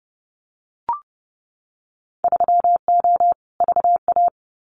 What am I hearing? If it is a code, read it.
Morse code, heard as a keyed tone, about 30 wpm, 700 Hz. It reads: 3O VA
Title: Winter Field Day Exchanges 30wpm